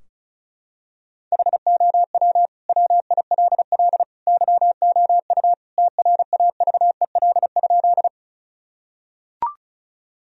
Morse code, heard as a keyed tone, about 36 wpm, 700 Hz. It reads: HOW WILL YOU TRAVEL?
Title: Sentences from Top 1000 Words - Review Part 2 - 35wpm